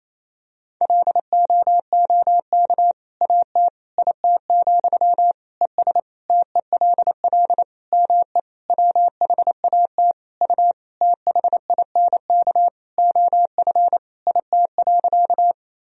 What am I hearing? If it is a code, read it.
Morse code, heard as a keyed tone, about 28 wpm, 700 Hz. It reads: LOOK AT IT, ES TELL ME WHAT U THINK OF IT.
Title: Sentences from Top 200 Words and Abbreviations 28wpm